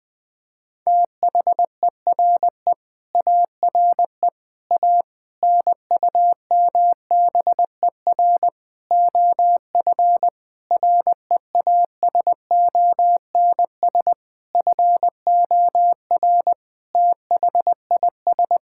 Morse code, heard as a keyed tone, about 20 wpm, 700 Hz. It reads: THERE ARE A NUMBER OF REASONS FOR THIS
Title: Sentences with 8-Word Phrases - 20wpm